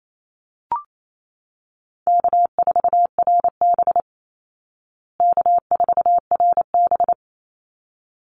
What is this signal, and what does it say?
Morse code, heard as a keyed tone, about 28 wpm, 700 Hz. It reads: K4RB K4RB
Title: Easy Call Signs Repeated 2x - 28wpm